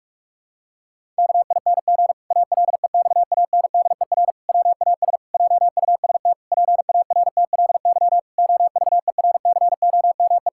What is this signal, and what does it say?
Morse code, heard as a keyed tone, about 46 wpm, 700 Hz. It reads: KING ALEXANDER WAS JUST PARTLY OVERCOME
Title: Pangram Challenge v2 - 45wpm